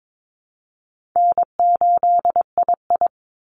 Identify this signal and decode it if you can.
Morse code, heard as a keyed tone, about 22 wpm, 700 Hz. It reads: N8II
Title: Easy Call Signs Repeated 2x - 22wpm